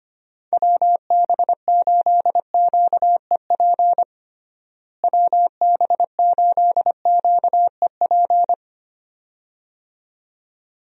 Morse code, heard as a keyed tone, about 25 wpm, 700 Hz. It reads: WB8QEP WB8QEP
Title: USA Call Signs - 6 Characters - Repeated 2x - 25wpm